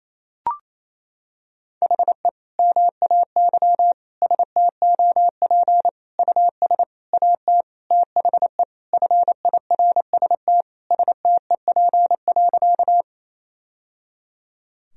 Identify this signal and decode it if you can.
Morse code, heard as a keyed tone, about 28 wpm, 700 Hz. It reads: HE MAY STOP US AT THE FIRST STEP.
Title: Sentences from Top 1000 Words - Review Part 2 - 28wpm